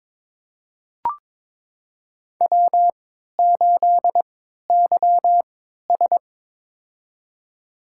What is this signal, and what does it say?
Morse code, heard as a keyed tone, about 22 wpm, spaced as for 13 wpm, 700 Hz. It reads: W8YS
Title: USA Call Signs - 4 Characters - 3x Character Spacing - 22wpm